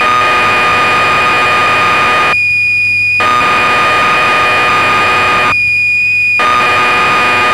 Cassette data (Kansas City type).
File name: magazines / Acorn User / 1984-11 BBC Programs